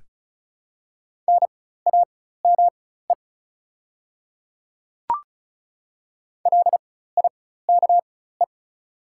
Morse code, heard as a keyed tone, about 36 wpm, spaced as for 16 wpm, 700 Hz. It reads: NAME LIKE
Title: 4x Character Spacing - Top 300 Words - 35wpm